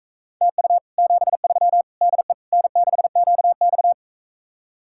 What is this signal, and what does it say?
Morse code, heard as a keyed tone, about 45 wpm, 700 Hz. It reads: TU 73 DE N6QX